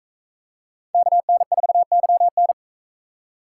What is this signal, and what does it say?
Morse code, heard as a keyed tone, about 42 wpm, 700 Hz. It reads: KN4YN